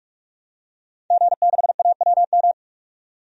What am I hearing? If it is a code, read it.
Morse code, heard as a keyed tone, about 43 wpm, 700 Hz. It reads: G6AWM
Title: DXCC-100 Call Signs - 5 Characters - 45wpm